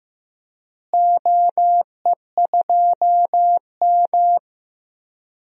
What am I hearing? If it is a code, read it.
Morse code, heard as a keyed tone, about 16 wpm, 700 Hz. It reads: OE2M
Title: DXCC-100 Call Signs - 4 Characters - Repeated 3x - 15wpm